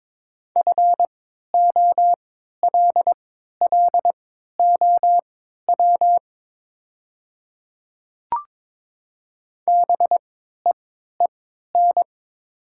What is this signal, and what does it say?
Morse code, heard as a keyed tone, about 22 wpm, spaced as for 12 wpm, 700 Hz. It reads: FOLLOW BEEN
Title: 3x Character Spacing - Top 300 Words - 22wpm